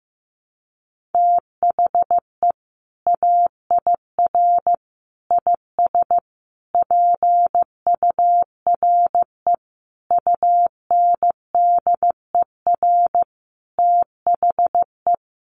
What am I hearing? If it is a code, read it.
Morse code, heard as a keyed tone, about 15 wpm, 700 Hz. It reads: THE AIR IS PURE UNDER THE
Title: Sentences from Top 1500 Words 15wpm